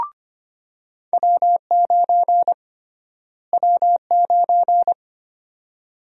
Morse code, heard as a keyed tone, about 25 wpm, 700 Hz. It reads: W9 W9
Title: USA Call Sign Prefixes - Repeated 2x - 25wpm